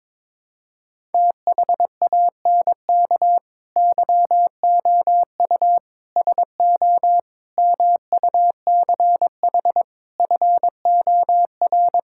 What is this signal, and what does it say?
Morse code, heard as a keyed tone, about 22 wpm, 700 Hz. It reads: THANK YOU SO MUCH FOR
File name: practice / 22wpm / Build A Phrase - 5-Word Phrases - 22wpm